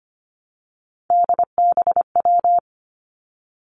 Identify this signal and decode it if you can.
Morse code, heard as a keyed tone, about 25 wpm, 700 Hz. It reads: DBW
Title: Three Challenging Letter-Number Combinations 25wpm